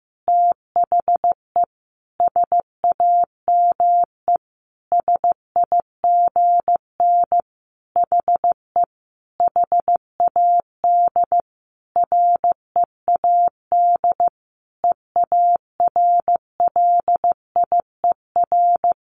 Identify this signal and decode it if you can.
Morse code, heard as a keyed tone, about 15 wpm, 700 Hz. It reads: THE SAME SIGN HE HAD READ EARLIER
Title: A Sound Of Thunder 15wpm